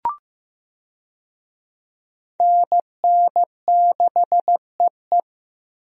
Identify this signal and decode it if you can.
Morse code, heard as a keyed tone, about 15 wpm, 700 Hz. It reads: NN6EE
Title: USA Call Signs - 5 Characters - Repeated 2x - 15wpm